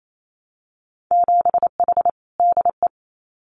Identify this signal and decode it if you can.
Morse code, heard as a keyed tone, about 28 wpm, 700 Hz. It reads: 7H DE
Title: Winter Field Day Exchanges Encore 28wpm